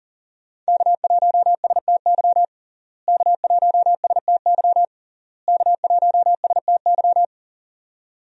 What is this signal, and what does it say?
Morse code, heard as a keyed tone, about 40 wpm, 700 Hz. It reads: K1STY K1STY K1STY
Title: USA Call Signs - 5 Characters - Repeated 3x - 40wpm